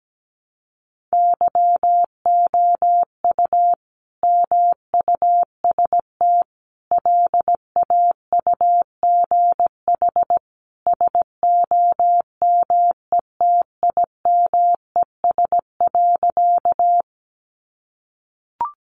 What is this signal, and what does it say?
Morse code, heard as a keyed tone, about 17 wpm, 700 Hz. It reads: YOU MUST LAUGH SOMETIMES.
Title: Sentences from Top 600 Words 17wpm